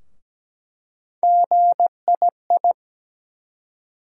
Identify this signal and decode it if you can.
Morse code, heard as a keyed tone, about 17 wpm, 700 Hz. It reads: GII
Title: USA Call Sign Postfixes - 3 Letters 17wpm